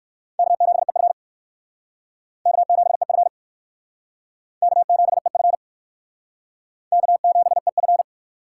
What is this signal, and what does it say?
Morse code, heard as a keyed tone, about 60 wpm, 700 Hz. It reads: K7EF K7EF K7EF K7EF